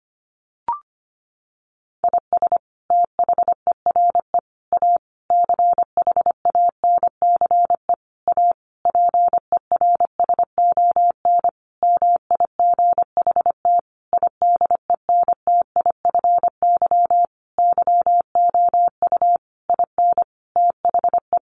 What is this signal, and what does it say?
Morse code, heard as a keyed tone, about 25 wpm, 700 Hz. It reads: IS THERE A CHANCE A PERSON MIGHT IDENTIFY YOU IN THE
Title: Sentences from Top 900 Words 25wpm